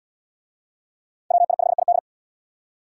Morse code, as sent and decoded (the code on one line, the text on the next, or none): .-- .. ..... .. -..
WI5ID